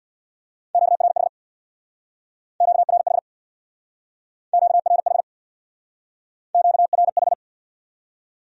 Morse code, heard as a keyed tone, about 61 wpm, 700 Hz. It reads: QRH QRH QRH QRH
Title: Speed-Racing Q-Codes - 50wpm